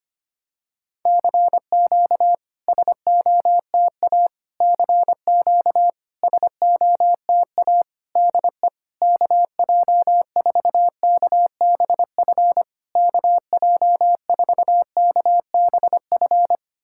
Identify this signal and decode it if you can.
Morse code, heard as a keyed tone, about 25 wpm, 700 Hz. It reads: CQ SOTA CQ SOTA DE KJ4KBF KJ4KBF
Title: SOTA QSO Practice 25wpm